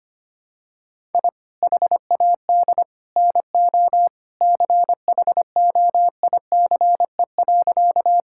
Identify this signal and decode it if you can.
Morse code, heard as a keyed tone, about 25 wpm, 700 Hz. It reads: I HAD NO CHOICE.
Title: Sentences from Top 1000 Words - Review Part 6 - 25wpm